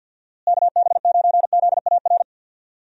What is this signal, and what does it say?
Morse code, heard as a keyed tone, about 52 wpm, 700 Hz. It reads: KB9ZAR
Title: USA Call Signs - 6 Characters - Repeated 3x - 50wpm